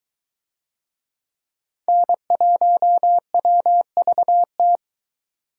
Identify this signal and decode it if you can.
Morse code, heard as a keyed tone, about 21 wpm, 700 Hz. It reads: N1WVT